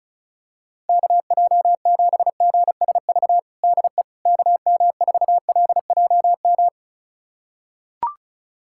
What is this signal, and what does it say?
Morse code, heard as a keyed tone, about 36 wpm, 700 Hz. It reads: KJ7GSV DE KM4LJM
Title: USA Callsign DE Callsign - 6 Characters - 35wpm